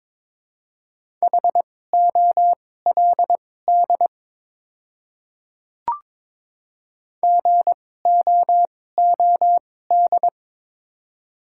Morse code, heard as a keyed tone, about 22 wpm, spaced as for 16 wpm, 700 Hz. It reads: HOLD GOOD
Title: 2x Character Spacing - Top 500 Words - 22wpm